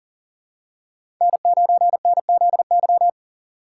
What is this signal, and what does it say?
Morse code, heard as a keyed tone, about 39 wpm, 690 Hz. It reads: N9NZY
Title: USA Call Signs - 5 Characters - 3x Character Spacing - 40wpm